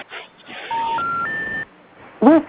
Special information tone, with noise.